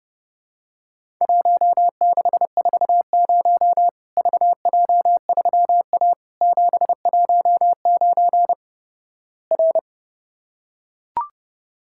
Morse code, heard as a keyed tone, about 30 wpm, 700 Hz. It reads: 1640 VJ3A 719 R
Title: Worked All Europe DX Contest - QTC Exchange - 30wpm